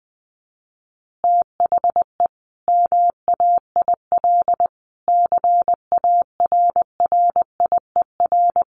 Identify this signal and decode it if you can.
Morse code, heard as a keyed tone, about 19 wpm, 700 Hz. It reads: THE MAIL CARRIER
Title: Additional Easy Phrases 20wpm